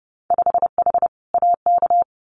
Morse code, heard as a keyed tone, about 30 wpm, 700 Hz. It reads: 5H AK
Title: Winter Field Day Exchanges Encore 30wpm